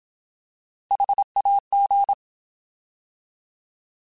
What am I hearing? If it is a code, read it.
Morse code, heard as a keyed tone, about 27 wpm, 800 Hz. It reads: HAG
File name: Alphabetical / Alphabetical mp3s / Hs @25